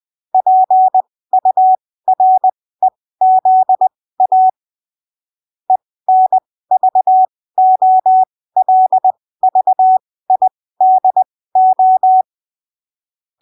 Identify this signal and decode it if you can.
Morse code, heard as a keyed tone, about 20 wpm, spaced as for 15 wpm, 750 Hz. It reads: PUREZA ENVOLVIDO
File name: arquivos / texto / texto 11 20ppm